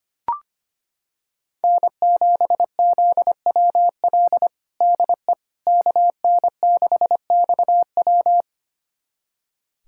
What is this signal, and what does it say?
Morse code, heard as a keyed tone, about 25 wpm, 700 Hz. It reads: N7ZWL DE KN6XW
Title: USA Callsign DE Callsign - 5 Characters - 25wpm